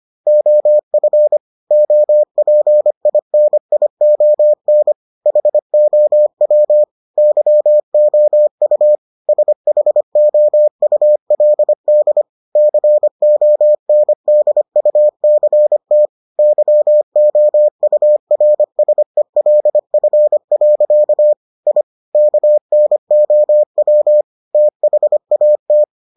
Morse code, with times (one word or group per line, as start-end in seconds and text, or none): 0.3-1.4 OF
1.7-4.9 OPINION
5.3-6.8 HOW
7.2-9.0 YOU
9.3-12.2 SHOULD
12.6-16.1 CONDUCT
16.4-21.3 YOURSELF.
21.7-21.8 I
22.1-24.2 KNOW
24.5-25.8 THAT